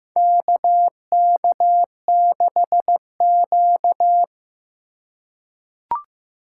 Morse code, heard as a keyed tone, about 15 wpm, 700 Hz. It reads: KK6Q
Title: USA Call Signs - 4 Characters - 2x Character Spacing - 15wpm